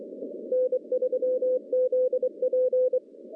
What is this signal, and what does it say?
Morse code, heard as a keyed tone, about 24 wpm, 510 Hz. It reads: N3ZP
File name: contests / CW-SS-k5tr- -1200